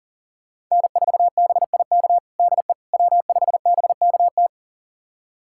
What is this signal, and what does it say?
Morse code, heard as a keyed tone, about 40 wpm, 700 Hz. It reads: N4BIK DE W5BKT
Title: USA Callsign DE Callsign - 5 Characters - 40wpm